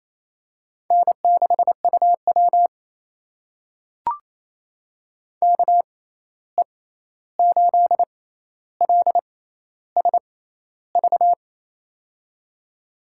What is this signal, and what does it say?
Morse code, heard as a keyed tone, about 28 wpm, spaced as for 13 wpm, 700 Hz. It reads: N6UW KE8LSV
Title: USA Call Signs - 6x Character Spacing - 28wpm